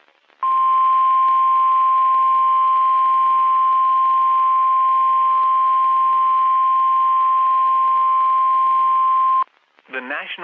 Emergency Alert System (SAME) tones (the 1050 Hz attention tone). It is heard from 0.4 to 9.5 s.
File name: eas messages / EAS WXM31 Elmira